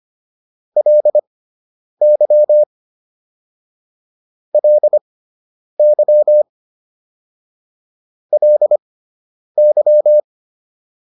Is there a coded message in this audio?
Yes: Morse code from 0.8 to 10.2 s, heard as a keyed tone, about 25 wpm, spaced as for 9 wpm, 600 Hz. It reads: LY LY LY